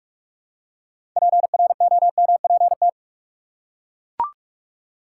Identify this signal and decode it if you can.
Morse code, heard as a keyed tone, about 42 wpm, 700 Hz. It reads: PROMPT
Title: Six Letter Words - 45wpm